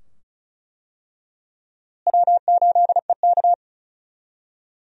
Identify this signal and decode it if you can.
Morse code, heard as a keyed tone, about 34 wpm, 700 Hz. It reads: W8EK
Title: USA Call Signs - 4 Characters - Repeated 2x - 35wpm